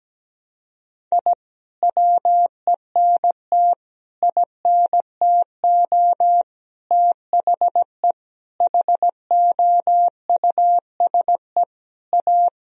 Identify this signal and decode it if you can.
Morse code, heard as a keyed tone, about 17 wpm, 700 Hz. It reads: I WENT INTO THE HOUSE A
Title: Sentences from Top 1000 Words - Review Part 1 - 17wpm